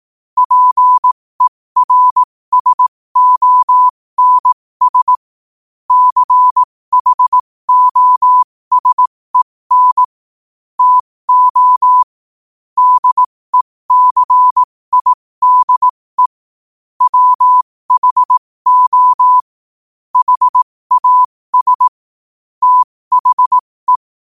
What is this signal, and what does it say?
Morse code, heard as a keyed tone, about 18 wpm, spaced as for 15 wpm, 1000 Hz. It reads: PERSONS CHOSEN TO DECIDE WHO HAS THE
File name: qotdfiles / 15wpm